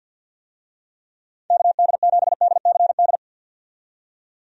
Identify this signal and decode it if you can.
Morse code, heard as a keyed tone, about 47 wpm, 700 Hz. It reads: KD7DCD
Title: USA Call Signs - 6 Characters - 50wpm